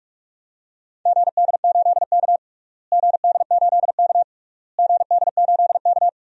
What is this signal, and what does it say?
Morse code, heard as a keyed tone, about 41 wpm, 700 Hz. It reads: GD8K GD8K GD8K
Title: DXCC-100 Call Signs - 4 Characters - Repeated 3x - 45wpm